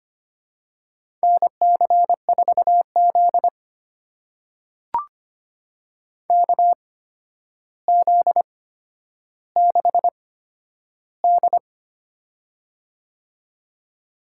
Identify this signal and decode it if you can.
Morse code, heard as a keyed tone, about 25 wpm, spaced as for 11 wpm, 700 Hz. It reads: NC4Z KZ6D
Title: USA Call Signs - 4 Characters - 8x Character Spacing - 25wpm